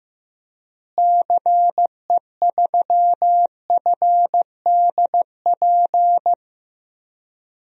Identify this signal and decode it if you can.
Morse code, heard as a keyed tone, about 16 wpm, 700 Hz. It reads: CE3FDP
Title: DXCC-100 Call Signs - 6 Characters - Repeated 2x - 15wpm